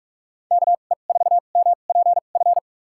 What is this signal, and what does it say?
Morse code, heard as a keyed tone, about 43 wpm, spaced as for 32 wpm, 700 Hz. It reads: KE4MPF
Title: USA Call Signs - 6 Characters - 2x Character Spacing - 45wpm